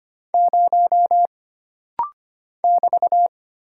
Morse code, heard as a keyed tone, about 25 wpm, 700 Hz. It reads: 0 <BT>
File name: Single Character (Letter:Number:Punctuation:Prosign) - Rapid-Fire - 25wpm